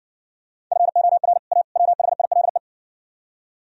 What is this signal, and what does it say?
Morse code, heard as a keyed tone, about 69 wpm, 700 Hz. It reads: FOR A WHILE